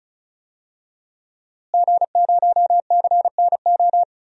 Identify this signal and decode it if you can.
Morse code, heard as a keyed tone, about 36 wpm, 700 Hz. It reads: G0CNO
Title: DXCC-100 Call Signs - 5 Characters - Repeated 3x - 35wpm